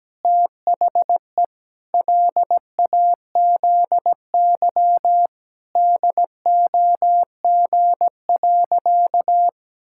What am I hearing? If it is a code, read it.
Morse code, heard as a keyed tone, about 17 wpm, 700 Hz. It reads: THE LAZY DOG.
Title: Pangram Challenge v2 - 17wpm